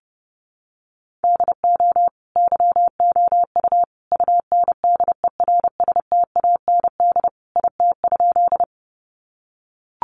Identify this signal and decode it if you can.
Morse code, heard as a keyed tone, about 30 wpm, 700 Hz. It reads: DO YOU UNDERSTAND IT?
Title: Sentences from Top 600 Words 30wpm